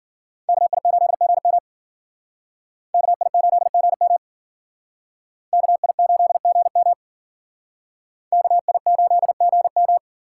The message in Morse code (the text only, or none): KI8GM KI8GM KI8GM KI8GM